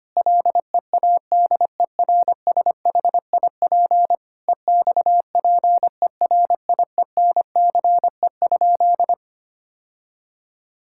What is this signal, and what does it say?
Morse code, heard as a keyed tone, about 25 wpm, 700 Hz. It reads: LEADERSHIP EXPERIENCE?
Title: Sentences from Top 1600 Words 25wpm